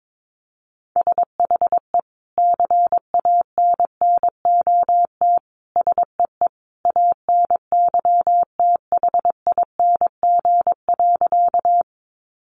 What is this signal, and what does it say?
Morse code, heard as a keyed tone, about 22 wpm, 700 Hz. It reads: SHE CANNOT SEE ANYTHING.
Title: Additional Easy Phrases 22wpm